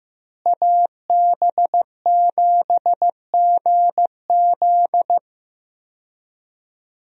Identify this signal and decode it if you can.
Morse code, heard as a keyed tone, about 15 wpm, 700 Hz. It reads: AB7GZ
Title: USA Call Signs - 5 Characters - Repeated 2x - 15wpm